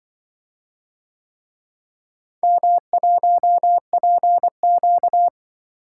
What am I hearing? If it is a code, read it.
Morse code, heard as a keyed tone, about 24 wpm, 700 Hz. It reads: M1PQ
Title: Speed-Racing - Four Letter-Number Combinations - 20wpm